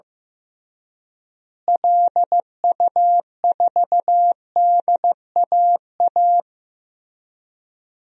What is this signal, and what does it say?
Morse code, heard as a keyed tone, about 16 wpm, 700 Hz. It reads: LU4DAA
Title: DXCC-100 Call Signs - 6 Characters - Repeated 3x - 15wpm